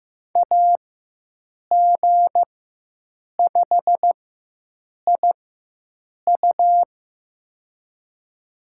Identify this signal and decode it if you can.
Morse code, heard as a keyed tone, about 15 wpm, spaced as for 7 wpm, 700 Hz. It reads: AG5IU
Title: USA Call Signs - 5 Characters - 4x Character Spacing - 15wpm